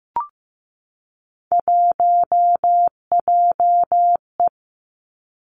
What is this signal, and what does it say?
Morse code, heard as a keyed tone, about 15 wpm, 700 Hz. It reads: 1JE
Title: Three Challenging Letter-Number Combinations 15wpm